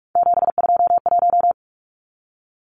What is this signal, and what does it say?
Morse code, heard as a keyed tone, about 44 wpm, 700 Hz. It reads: 721
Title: Numbers - 3-Digit 45wpm